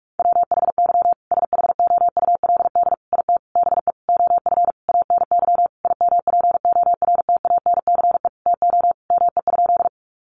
Morse code, heard as a keyed tone, about 44 wpm, 700 Hz. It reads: WHY SHOULD IT BE OF ANY IMPORTANCE TO ME?
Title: Sentences from Top 1200 Words 45wpm